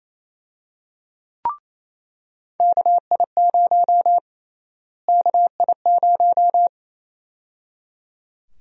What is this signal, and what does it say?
Morse code, heard as a keyed tone, about 27 wpm, 700 Hz. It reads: KI0 KI0
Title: USA Call Sign Prefixes - Repeated 2x - 28wpm